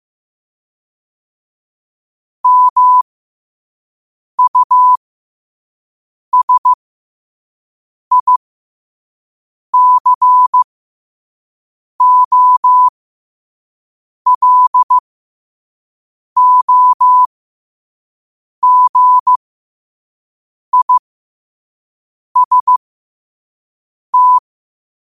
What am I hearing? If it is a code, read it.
Morse code, heard as a keyed tone, about 15 wpm, spaced as for 5 wpm, 1000 Hz. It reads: MUSICOLOGIST